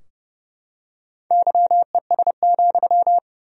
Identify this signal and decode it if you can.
Morse code, heard as a keyed tone, about 30 wpm, 700 Hz. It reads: YES,
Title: 2x Word-Spacing - Sentences from Top 500 Words - 30wpm